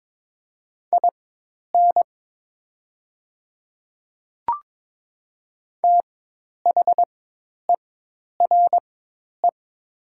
Morse code, heard as a keyed tone, about 22 wpm, spaced as for 10 wpm, 690 Hz. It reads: IN THERE
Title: 4x Character Spacing - Top 100 Words - 22wpm